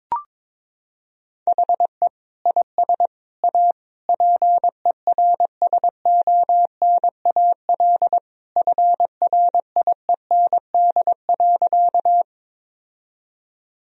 Morse code, heard as a keyed tone, about 22 wpm, 700 Hz. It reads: HE IS A PERSONAL FRIEND.
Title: Sentences from Top 1000 Words - Review Part 4 - 22wpm